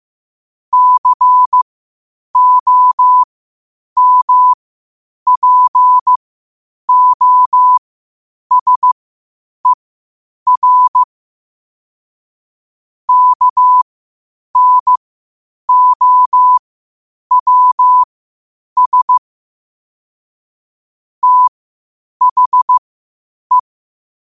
Morse code, heard as a keyed tone, about 15 wpm, spaced as for 8 wpm, 1000 Hz. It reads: COMPOSER KNOWS THE
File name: qotdfiles / 8wpm